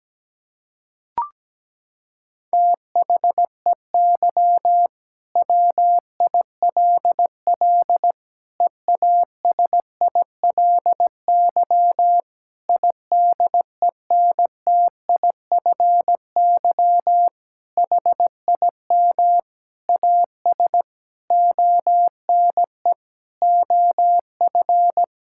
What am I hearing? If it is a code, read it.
Morse code, heard as a keyed tone, about 17 wpm, 700 Hz. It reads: THEY WILL EASILY IDENTIFY HIM AS ONE OF
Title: Sentences from Top 1000 Words - Review Part 6 - 17wpm